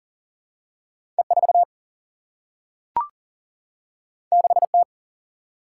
Morse code, heard as a keyed tone, about 38 wpm, 700 Hz. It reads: E 4 6 T
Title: Single-Letter Single-Number 40wpm